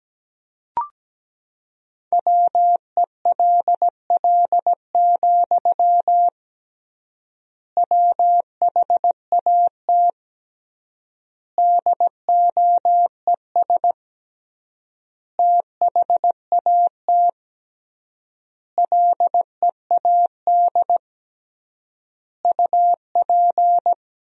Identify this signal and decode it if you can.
Morse code, heard as a keyed tone, about 17 wpm, 700 Hz. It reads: WELL, WHAT DOES THAT LEAD UP
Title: 3x Word-Spacing - Sentences from Top 500 Words - 17wpm